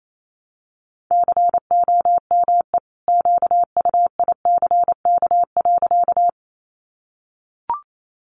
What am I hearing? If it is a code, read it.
Morse code, heard as a keyed tone, about 28 wpm, 700 Hz. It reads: COME QUICK.
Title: Sentences from Top 500 Words 28wpm